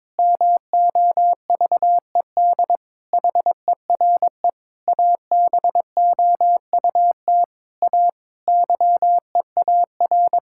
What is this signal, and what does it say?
Morse code, heard as a keyed tone, about 22 wpm, 700 Hz. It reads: MOVED HERE ABOUT A YEAR
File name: Sentences with 7-Word Phrases - 22wpm